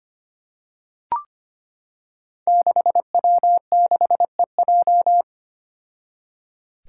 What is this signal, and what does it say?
Morse code, heard as a keyed tone, about 25 wpm, 700 Hz. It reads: 6W6EJ
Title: Five Challenging Letter-Number Combinations 25wpm